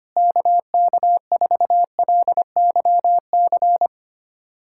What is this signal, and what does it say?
Morse code, heard as a keyed tone, about 25 wpm, 700 Hz. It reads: KK4LYC
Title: USA Callsign DE Callsign - 6 Characters - 25wpm